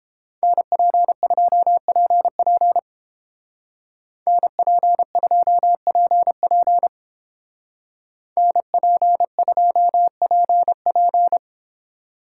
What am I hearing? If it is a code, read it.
Morse code, heard as a keyed tone, about 30 wpm, 700 Hz. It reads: NP2PP NP2PP NP2PP